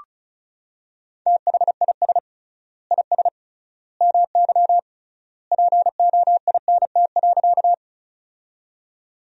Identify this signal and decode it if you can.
Morse code, heard as a keyed tone, about 36 wpm, 700 Hz. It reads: THIS IS MY POINT.
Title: 3x Word-Spacing - Sentences from Top 200 Words - 35wpm